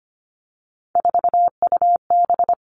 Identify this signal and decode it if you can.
Morse code, heard as a keyed tone, about 25 wpm, 700 Hz. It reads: 4UB
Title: Three Challenging Letter-Number Combinations 25wpm